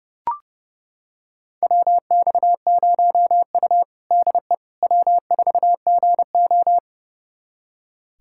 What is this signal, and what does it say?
Morse code, heard as a keyed tone, about 30 wpm, 700 Hz. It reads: WX0U DE W4GO
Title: USA Callsign DE Callsign - 4 Characters - 30wpm